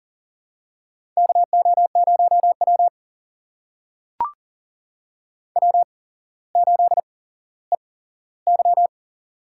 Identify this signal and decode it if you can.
Morse code, heard as a keyed tone, about 42 wpm, spaced as for 17 wpm, 700 Hz. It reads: KO0W W8EY
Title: USA Call Signs - 4 Characters - 8x Character Spacing - 40wpm